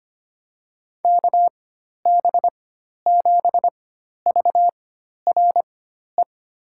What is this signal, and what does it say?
Morse code, heard as a keyed tone, about 25 wpm, spaced as for 12 wpm, 700 Hz. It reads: KB7VRE
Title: USA Call Signs - 6 Characters - 4x Character Spacing - 25wpm